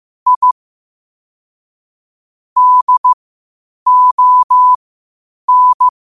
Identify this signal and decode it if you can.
Morse code, heard as a keyed tone, about 15 wpm, spaced as for 8 wpm, 1000 Hz. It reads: I DON
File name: qotdfiles / 8wpm